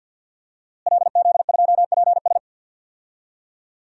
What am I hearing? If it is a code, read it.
Morse code, heard as a keyed tone, about 52 wpm, 700 Hz. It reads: LZ2PS
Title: DXCC-100 Call Signs - 5 Characters - Repeated 2x - 50wpm